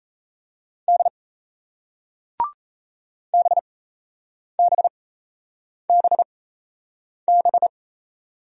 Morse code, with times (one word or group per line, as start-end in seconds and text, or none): 0.9-1.1 D
3.3-7.7 BBBB